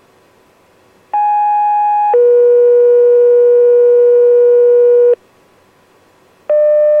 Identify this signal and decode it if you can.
Two-tone page, heard as a one-second tone, then a three-second tone, about 830 then 480 Hz. It is heard from 1.1 to 5.1 s.